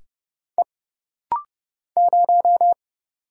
Morse code, heard as a keyed tone, about 30 wpm, 700 Hz. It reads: E 0
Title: Single Letter-Number - Rapid-Fire 30wpm